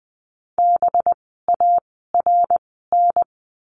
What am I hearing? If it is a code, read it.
Morse code, heard as a keyed tone, about 20 wpm, spaced as for 15 wpm, 700 Hz. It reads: BARN
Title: 185 Lesson 30f - Letter b - Words 20wpm